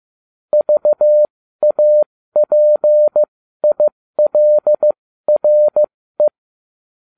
Morse code, heard as a keyed tone, about 15 wpm, spaced as for 13 wpm, 600 Hz. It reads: VAPILRE